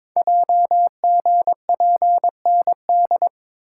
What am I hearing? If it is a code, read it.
Morse code, heard as a keyed tone, about 22 wpm, 700 Hz. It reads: JGPND
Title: Code Groups - 5 Letters - 22wpm